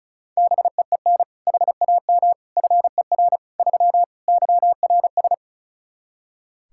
Morse code, heard as a keyed tone, about 34 wpm, 700 Hz. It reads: BEEN HAM FER 3 YRS
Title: K1USN SST QSO Practice 35wpm